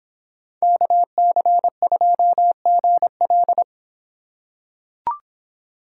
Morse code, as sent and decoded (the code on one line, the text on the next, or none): -.- -.-. ..--- --. .-..
KC2GL